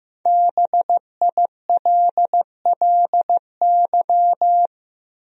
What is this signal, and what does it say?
Morse code, heard as a keyed tone, about 16 wpm, 700 Hz. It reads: BILLY
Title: USA Common Names - 15wpm